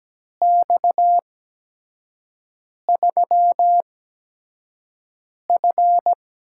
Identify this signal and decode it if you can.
Morse code, heard as a keyed tone, about 17 wpm, spaced as for 5 wpm, 700 Hz. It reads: X3F
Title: USA Call Signs - 4 Characters - 8x Character Spacing - 17wpm